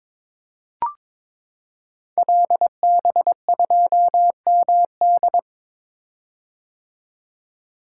Morse code, heard as a keyed tone, about 22 wpm, 700 Hz. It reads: LB2MD
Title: Code Groups - 5 Letter:Numbers - 22wpm